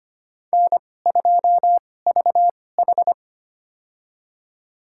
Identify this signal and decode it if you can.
Morse code, heard as a keyed tone, about 25 wpm, spaced as for 18 wpm, 700 Hz. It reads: N2VH
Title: USA Call Signs - 4 Characters - 2x Character Spacing - 25wpm